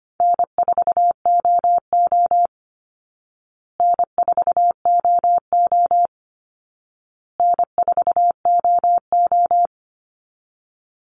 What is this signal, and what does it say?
Morse code, heard as a keyed tone, about 25 wpm, 700 Hz. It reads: N4OO N4OO N4OO
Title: Call Signs Repeated 2x - 25wpm